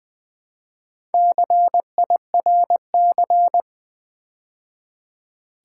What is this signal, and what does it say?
Morse code, heard as a keyed tone, about 20 wpm, 700 Hz. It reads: CIRC
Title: Build A Word - 2 Letters At A Time - 6-Letter Words - 20wpm